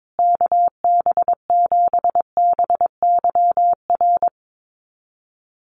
Morse code, heard as a keyed tone, about 22 wpm, 700 Hz. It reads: KB7BYR
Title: USA Call Signs Repeated 2x 22wpm